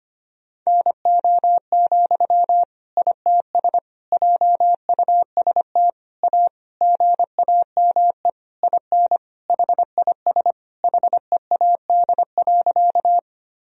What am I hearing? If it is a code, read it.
Morse code, heard as a keyed tone, about 25 wpm, 700 Hz. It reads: NO, ITS JUST A GAME IN HIS HEAD.
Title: Sentences from Top 1000 Words - Review Part 3 - 25wpm